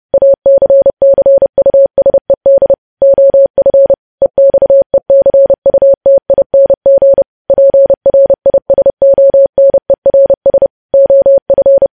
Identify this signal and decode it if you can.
Morse code, heard as a keyed tone, about 30 wpm, 550 Hz. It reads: ACCUSED OF EXECUTING PRISONERS OF
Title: News Headlines at 30